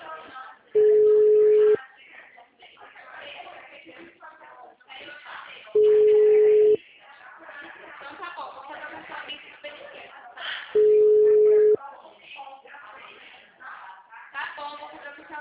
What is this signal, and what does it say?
Ringback tone (Europe, 425 Hz), with speech.
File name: out- -2010- - - .55630